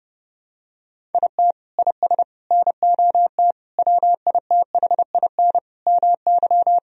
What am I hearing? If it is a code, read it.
Morse code, heard as a keyed tone, about 30 wpm, 700 Hz. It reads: IT IS NOT WITHIN MY
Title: Sentences from Top 1000 Words - Review Part 4 - 30wpm